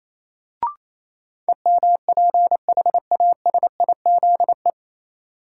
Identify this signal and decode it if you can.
Morse code, heard as a keyed tone, about 28 wpm, 700 Hz. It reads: EMPHASIZE
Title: Sentences from Top 1500 Words - New Word Drill Alt - 28wpm